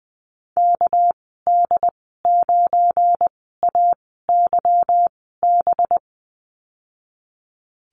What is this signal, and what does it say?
Morse code, heard as a keyed tone, about 20 wpm, spaced as for 15 wpm, 700 Hz. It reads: KD9AYB
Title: 213 Lesson 34g - Number 8 - Call Signs 20wpm